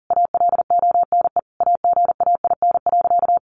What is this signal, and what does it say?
Morse code, heard as a keyed tone, about 40 wpm, 700 Hz. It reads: ALONE AGAIN.
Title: Sentences from Top 1000 Words 40wpm